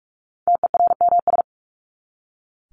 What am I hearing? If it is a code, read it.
Morse code, heard as a keyed tone, about 44 wpm, 700 Hz. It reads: TERMS
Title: ES:S Suffix - Top 100 Words with Base Words 45wpm